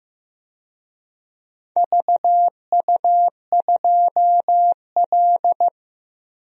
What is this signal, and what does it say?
Morse code, heard as a keyed tone, about 15 wpm, 700 Hz. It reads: VU2L
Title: DXCC-100 Call Signs - 4 Characters - 15wpm